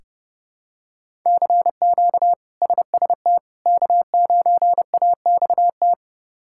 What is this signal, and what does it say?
Morse code, heard as a keyed tone, about 32 wpm, 700 Hz. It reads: CQ SST K9AXT
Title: K1USN SST QSO Practice - 30wpm